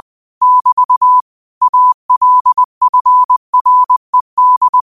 Morse code, heard as a keyed tone, about 20 wpm, 1000 Hz. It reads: <BT> ALFRED
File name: qotdfiles / 20wpm